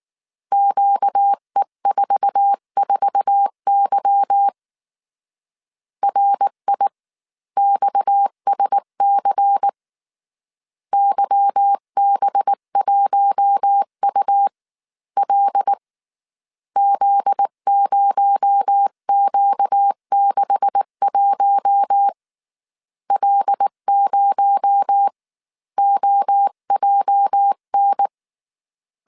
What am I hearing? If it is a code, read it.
Morse code, heard as a keyed tone, about 19 wpm, spaced as for 16 wpm, 800 Hz. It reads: QE44Y RIXSC YB1UL Z0Q61 L0OJN
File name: CW / DET / 1 9 1500C S B 18wpm 2551